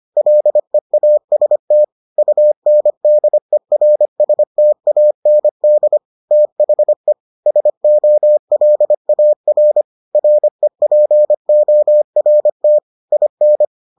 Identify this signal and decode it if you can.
Morse code, heard as a keyed tone, about 25 wpm, 600 Hz. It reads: LEAST UNDERSTAND THE SOLAR REPORT IN